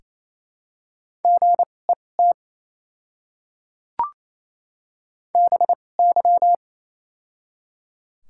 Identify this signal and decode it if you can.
Morse code, heard as a keyed tone, about 28 wpm, spaced as for 20 wpm, 700 Hz. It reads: GET BY
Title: 2x Character Spacing - Top 500 Words - 28wpm